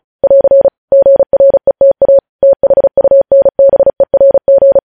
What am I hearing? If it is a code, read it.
Morse code, heard as a keyed tone, about 36 wpm, 550 Hz. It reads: <AR> GRETA THUNBERG
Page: news headlines in morse code at 35 wpm